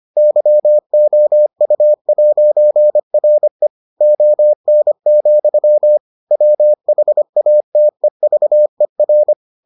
Morse code, heard as a keyed tone, about 25 wpm, 600 Hz. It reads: YOU'RE ON, WHATEVER